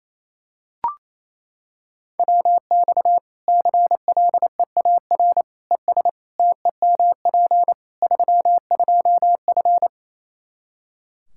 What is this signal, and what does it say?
Morse code, heard as a keyed tone, about 28 wpm, 700 Hz. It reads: WX CLEAR ES TEMP 32F
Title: Practice QSO - Exchange 2 - Info - 28wpm